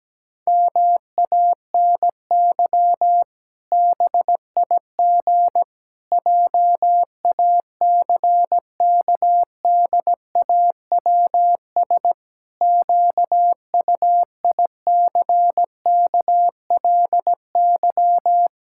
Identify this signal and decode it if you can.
Morse code, heard as a keyed tone, about 17 wpm, 700 Hz. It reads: MANY BIG JACKDAWS QUICKLY
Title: Pangram Challenge v2 - 17wpm